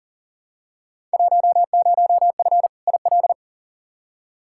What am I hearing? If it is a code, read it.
Morse code, heard as a keyed tone, about 37 wpm, 700 Hz. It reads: 10F IL
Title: Field Day Exchanges Encore - 40wpm